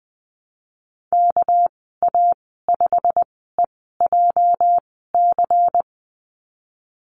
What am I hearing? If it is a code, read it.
Morse code, heard as a keyed tone, about 20 wpm, spaced as for 14 wpm, 700 Hz. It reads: KA5EJC
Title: 213 Lesson 34g - Number 8 - Call Signs 20wpm